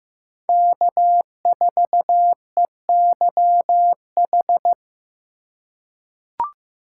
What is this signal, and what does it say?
Morse code, heard as a keyed tone, about 15 wpm, 700 Hz. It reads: K4EYH
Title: USA Call Signs - 2x Character Spacing - 15wpm